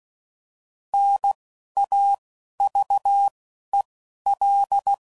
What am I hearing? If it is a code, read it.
Morse code, heard as a keyed tone, about 16 wpm, spaced as for 11 wpm, 800 Hz. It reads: NAVEL
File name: Prefixes / By First Letter / Ns